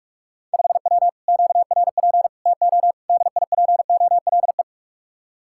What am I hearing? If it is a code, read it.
Morse code, heard as a keyed tone, about 46 wpm, 700 Hz. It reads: 5W QRP TO DIPOLE